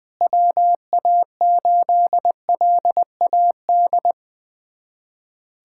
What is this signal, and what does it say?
Morse code, heard as a keyed tone, about 20 wpm, 700 Hz. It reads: WA8LAD
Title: USA Callsign DE Callsign 20wpm